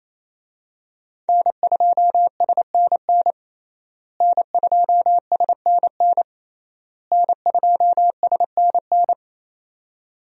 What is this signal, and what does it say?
Morse code, heard as a keyed tone, about 28 wpm, 700 Hz. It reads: N2SNN N2SNN N2SNN
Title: USA Call Signs - 5 Characters - Repeated 3x - 28wpm